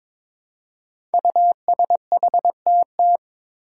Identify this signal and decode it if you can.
Morse code, heard as a keyed tone, about 22 wpm, 700 Hz. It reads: USHTT
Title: Five Challenging Letter-Number Combinations 22wpm